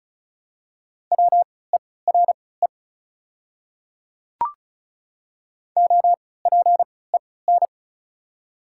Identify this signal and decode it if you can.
Morse code, heard as a keyed tone, about 36 wpm, spaced as for 20 wpm, 690 Hz. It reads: WERE OPEN
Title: 3x Character Spacing - Top 500 Words - 35wpm